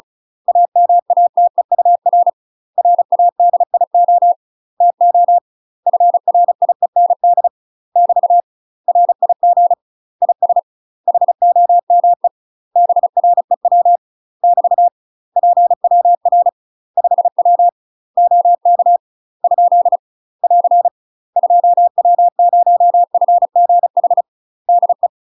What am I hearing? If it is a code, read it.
Morse code, heard as a keyed tone, about 35 wpm, 700 Hz. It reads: AMATEUR RADIO TO FRIEND <BT> RIG IS HOME BREW <BT> PWR 5W OK ? <AR> 2W0FGH DE
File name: UK MASTER QSOs TRAINING / 35WPM / UK MASTER QSOs 0039